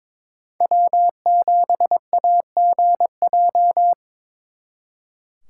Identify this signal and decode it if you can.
Morse code, heard as a keyed tone, about 26 wpm, 700 Hz. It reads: W7AGJ
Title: Speed Racing - USA 1x3 Call Signs 22wpm